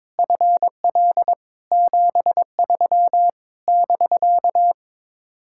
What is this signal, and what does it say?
Morse code, heard as a keyed tone, about 22 wpm, 700 Hz. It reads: FL 73 <BK>
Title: Speed-Racing Common QSO Phrases - 22wpm